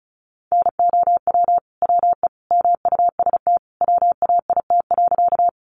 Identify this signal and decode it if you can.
Morse code, heard as a keyed tone, about 34 wpm, 700 Hz. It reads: NOW WE MUST WAIT.
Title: Sentences from Top 500 Words 35wpm